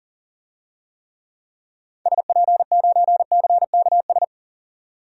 Morse code, heard as a keyed tone, about 40 wpm, 700 Hz. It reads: SP9CKS
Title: DXCC-100 Call Signs - 6 Characters - 40wpm